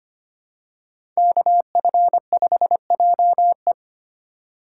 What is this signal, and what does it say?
Morse code, heard as a keyed tone, about 25 wpm, 700 Hz. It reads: KF5JE
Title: USA Call Signs - 5 Characters - 4x Character Spacing - 25wpm